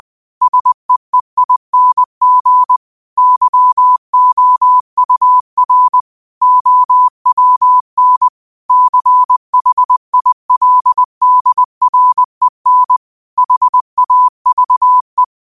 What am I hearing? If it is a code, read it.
Morse code, heard as a keyed tone, about 20 wpm, 1000 Hz. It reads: SEEING YOUR OWN CHILDREN HAVE